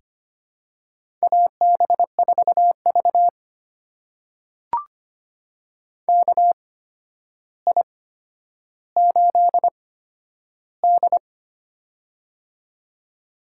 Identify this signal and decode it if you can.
Morse code, heard as a keyed tone, about 25 wpm, spaced as for 11 wpm, 700 Hz. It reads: AB4V KI8D
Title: USA Call Signs - 4 Characters - 8x Character Spacing - 25wpm